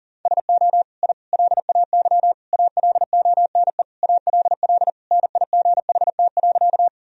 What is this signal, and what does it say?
Morse code, heard as a keyed tone, about 40 wpm, 700 Hz. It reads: SO I LAY ALONE ALL NIGHT.
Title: Sentences from Top 1000 Words - Review Part 2 - 40wpm